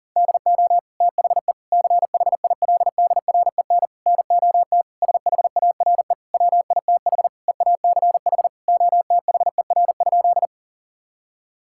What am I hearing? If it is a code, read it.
Morse code, heard as a keyed tone, about 40 wpm, 700 Hz. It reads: DO THE CHILDREN NOT SHARE WITH EACH OTHER?
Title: Sentences from Top 1000 Words - Review Part 6 - 40wpm